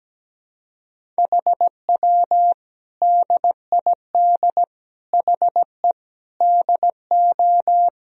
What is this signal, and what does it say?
Morse code, heard as a keyed tone, about 17 wpm, 700 Hz. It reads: HW DID HE DO
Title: Sentences from Top 100 Words and Abbreviations 17wpm